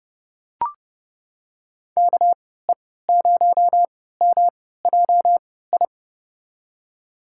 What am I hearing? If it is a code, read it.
Morse code, heard as a keyed tone, about 30 wpm, spaced as for 17 wpm, 700 Hz. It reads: KE0MJI
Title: USA Call Signs - 6 Characters - 3x Character Spacing - 30wpm